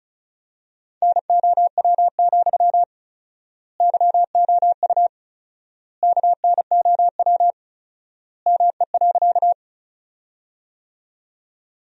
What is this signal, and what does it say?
Morse code, heard as a keyed tone, about 34 wpm, 700 Hz. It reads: NOW, YOU KNOW ME.
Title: 4x Word-Spacing - Sentences from Top 200 Words - 35wpm